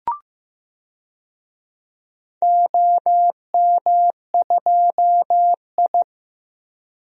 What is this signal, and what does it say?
Morse code, heard as a keyed tone, about 15 wpm, 700 Hz. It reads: OM2I
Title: DXCC-100 Call Signs - 4 Characters - Repeated 3x - 15wpm